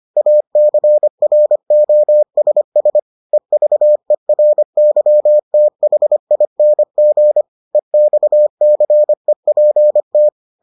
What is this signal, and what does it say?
Morse code, heard as a keyed tone, about 25 wpm, 600 Hz. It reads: ACROSS EVERYTHING EXCEPT